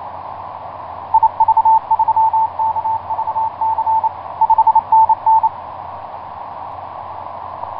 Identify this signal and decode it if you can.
Morse code, heard as a keyed tone, about 28 wpm, 890 Hz. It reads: IV3KVC 5NN
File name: vk9x-g6ay iv3kvc etc 12m cw 23feb2010 0803